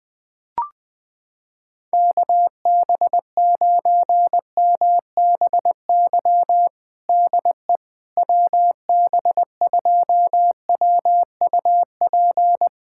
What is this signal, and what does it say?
Morse code, heard as a keyed tone, about 21 wpm, 700 Hz. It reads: KB9MBY DE WB2WUP
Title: USA Callsign DE Callsign - 6 Characters - 20wpm